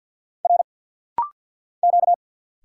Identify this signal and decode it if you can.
Morse code, heard as a keyed tone, about 48 wpm, 700 Hz. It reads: R Q
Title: Single Letters - Rapid-Fire 50wpm